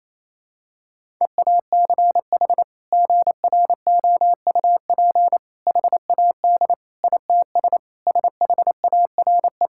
Morse code, heard as a keyed tone, about 28 wpm, 700 Hz. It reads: EACH GROUP HAD ITS SHARE
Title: Sentences from Top 1000 Words - Review Part 6 - 28wpm